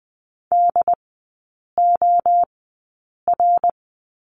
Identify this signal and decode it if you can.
Morse code, heard as a keyed tone, about 20 wpm, spaced as for 8 wpm, 690 Hz. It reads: DOR
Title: 087 Lesson 15e - Number 5 - Three Characters 20wpm